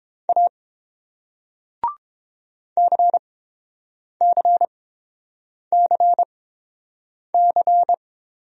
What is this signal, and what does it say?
Morse code, heard as a keyed tone, about 30 wpm, spaced as for 7 wpm, 700 Hz. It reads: A CCCC